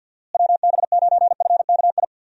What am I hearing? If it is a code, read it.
Morse code, heard as a keyed tone, about 51 wpm, 700 Hz. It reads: WB9FKI
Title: USA Call Signs - 6 Characters - Repeated 3x - 50wpm